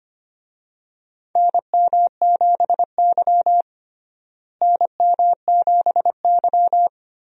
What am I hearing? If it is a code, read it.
Morse code, heard as a keyed tone, about 25 wpm, 700 Hz. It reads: NM7Y NM7Y
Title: USA Call Signs - 4 Characters - Repeated 2x - 25wpm